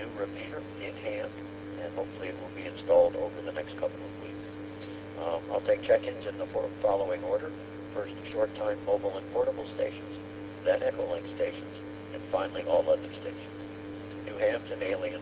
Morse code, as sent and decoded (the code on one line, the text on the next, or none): none